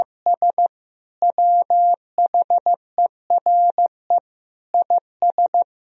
Morse code, heard as a keyed tone, about 16 wpm, 700 Hz. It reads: S WHERE IS
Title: Sentences from Top 300 Words and Abbreviations 15wpm